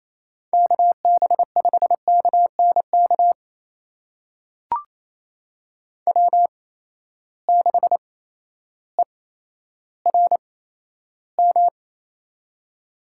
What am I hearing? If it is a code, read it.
Morse code, heard as a keyed tone, about 28 wpm, spaced as for 13 wpm, 700 Hz. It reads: KB5KNK W6ERM
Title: USA Call Signs - 8x Character Spacing - 28wpm